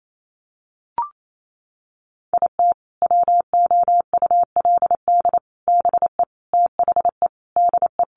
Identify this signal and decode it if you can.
Morse code, heard as a keyed tone, about 28 wpm, 700 Hz. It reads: IT WOULD BE THE DE
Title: Sentences from Top 1400 Words 28wpm